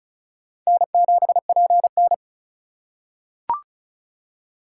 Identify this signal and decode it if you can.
Morse code, heard as a keyed tone, about 36 wpm, 700 Hz. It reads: N7PN
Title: USA Call Signs - 4 Characters - 2x Character Spacing - 35wpm